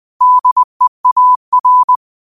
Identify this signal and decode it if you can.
Morse code, heard as a keyed tone, about 20 wpm, 1000 Hz. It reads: DEAR
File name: 20wpm